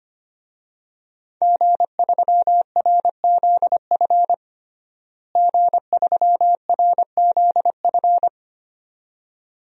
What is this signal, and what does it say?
Morse code, heard as a keyed tone, about 25 wpm, 700 Hz. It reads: G3RZF G3RZF
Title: DXCC-100 Call Signs - 5 Characters - Repeated 2x - 25wpm